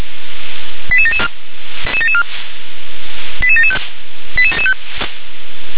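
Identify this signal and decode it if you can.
Five-tone selective call (ZVEI), heard as quick runs of five tones, tones of about 70 ms. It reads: 88774 88775
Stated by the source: ZVEI five-tone selective call 88771 received on SDR.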